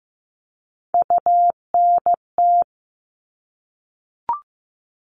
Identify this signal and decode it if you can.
Morse code, heard as a keyed tone, about 15 wpm, 700 Hz. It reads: UNT
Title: All Cut Numbers - 3-Digit 15wpm